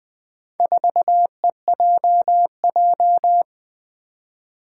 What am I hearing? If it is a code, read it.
Morse code, heard as a keyed tone, about 20 wpm, 700 Hz. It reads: 4EJJ
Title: Four Challenging Letter-Number Combinations 20wpm